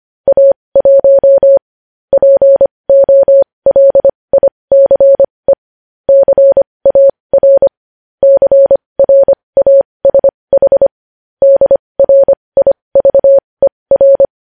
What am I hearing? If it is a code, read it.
Morse code, heard as a keyed tone, about 25 wpm, spaced as for 20 wpm, 550 Hz. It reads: A1 POLICE CAR CRASH DRIVER
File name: podcast / News headlines-1x-25x20